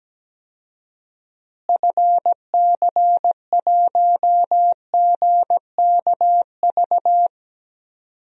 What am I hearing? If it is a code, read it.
Morse code, heard as a keyed tone, about 17 wpm, 700 Hz. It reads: FC1GKV
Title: DXCC-100 Call Signs - 6 Characters - Repeated 2x - 17wpm